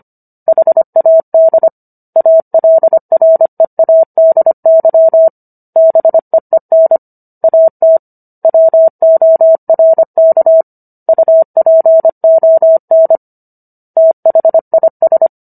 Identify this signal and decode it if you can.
Morse code, heard as a keyed tone, about 25 wpm, 650 Hz. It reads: HAD ALREADY BEEN AT WORK UPON THIS